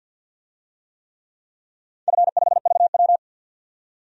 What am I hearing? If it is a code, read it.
Morse code, heard as a keyed tone, about 51 wpm, 700 Hz. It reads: F5VW
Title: DXCC-100 Call Signs - 4 Characters - 50wpm